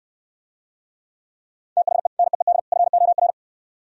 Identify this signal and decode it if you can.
Morse code, heard as a keyed tone, about 67 wpm, 700 Hz. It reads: THE NEED FOR